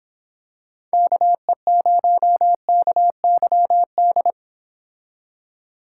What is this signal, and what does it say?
Morse code, heard as a keyed tone, about 26 wpm, 700 Hz. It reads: KE0KYD